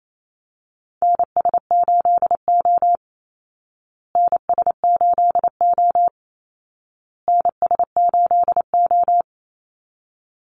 Morse code, heard as a keyed tone, about 28 wpm, 700 Hz. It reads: NS8O NS8O NS8O
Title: Call Signs Repeated 2x - 28wpm